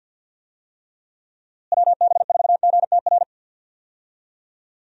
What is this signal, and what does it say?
Morse code, heard as a keyed tone, about 51 wpm, 700 Hz. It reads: WB4GTR
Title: USA Call Signs - 6 Characters - Repeated 3x - 50wpm